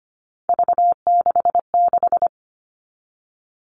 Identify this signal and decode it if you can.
Morse code, heard as a keyed tone, about 25 wpm, 700 Hz. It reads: V66
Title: All Cut Numbers - 3-Digit 25wpm